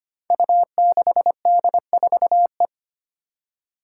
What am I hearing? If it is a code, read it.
Morse code, heard as a keyed tone, about 25 wpm, 700 Hz. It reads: U6D4E
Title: Five Challenging Letter-Number Combinations 25wpm